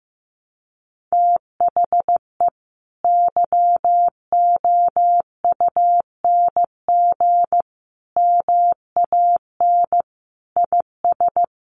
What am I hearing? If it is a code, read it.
Morse code, heard as a keyed tone, about 14 wpm, 700 Hz. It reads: THE YOUNG MAN IS
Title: Sentences from Top 1200 Words 15wpm